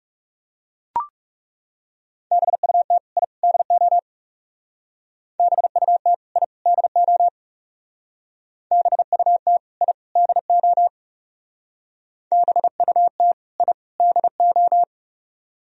Morse code, heard as a keyed tone, about 36 wpm, 700 Hz. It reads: BUT I DO BUT I DO BUT I DO BUT I DO
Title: Speed-Racing - Sets of 3 Words 30wpm